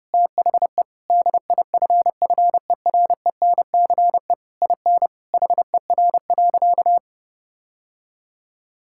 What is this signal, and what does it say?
Morse code, heard as a keyed tone, about 30 wpm, 700 Hz. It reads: THE DIFFERENCE IN HER.
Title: Sentences from Top 1000 Words - Review Part 5 - 30wpm